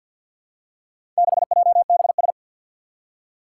Morse code, heard as a keyed tone, about 47 wpm, 700 Hz. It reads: 6JBS
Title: Four Challenging Letter-Number Combinations 50wpm